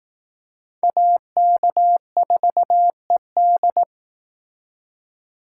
Morse code, heard as a keyed tone, about 18 wpm, 700 Hz. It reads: AK4ED
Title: Speed Racing - USA 2x2 Call Signs 15wpm